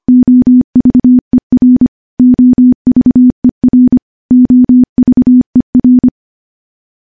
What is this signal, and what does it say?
Morse code, heard as a keyed tone, about 25 wpm, 250 Hz. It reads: OVER OVER OVER
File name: morse-training-25WPM